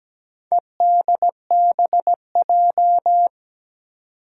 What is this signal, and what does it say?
Morse code, heard as a keyed tone, about 17 wpm, 700 Hz. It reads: EDBJ
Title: Four Challenging Letter-Number Combinations 17wpm